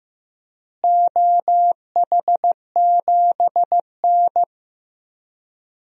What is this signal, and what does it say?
Morse code, heard as a keyed tone, about 15 wpm, 700 Hz. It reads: OH7N
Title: DXCC-100 Call Signs - 4 Characters - Repeated 3x - 15wpm